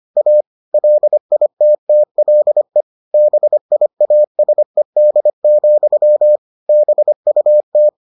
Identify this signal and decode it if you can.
Morse code, heard as a keyed tone, about 25 wpm, 600 Hz. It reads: A LITTLE BIASED, BUT